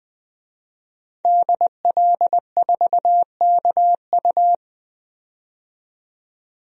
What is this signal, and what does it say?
Morse code, heard as a keyed tone, about 20 wpm, 700 Hz. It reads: DL4KU
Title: DXCC-100 Call Signs - 5 Characters - 20wpm